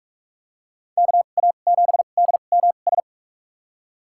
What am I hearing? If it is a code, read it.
Morse code, heard as a keyed tone, about 42 wpm, spaced as for 31 wpm, 700 Hz. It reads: KA7DMS
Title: USA Call Signs - 6 Characters - 2x Character Spacing - 45wpm